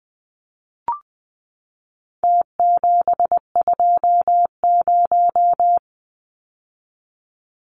Morse code, heard as a keyed tone, about 20 wpm, 700 Hz. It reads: T720
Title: Numbers - Contest Style - 4-Digit 20wpm